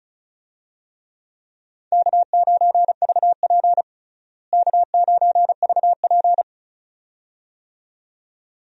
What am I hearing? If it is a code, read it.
Morse code, heard as a keyed tone, about 36 wpm, 700 Hz. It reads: K9VP K9VP
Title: USA Call Signs - 4 Characters - Repeated 2x - 35wpm